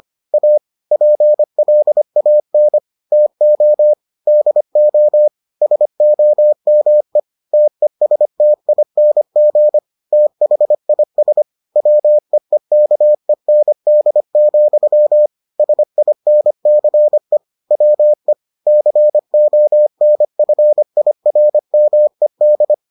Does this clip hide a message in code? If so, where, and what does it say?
Yes: Morse code from 0.3 to 22.7 s, heard as a keyed tone, about 25 wpm, 600 Hz. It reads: A PLAN TO DO SOME TESTING THIS WEEKEND, SINCE WE CONFIRMED